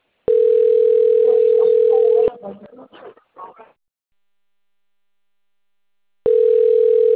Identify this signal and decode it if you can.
Ringback tone (North America), 440 + 480 Hz.
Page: Campinas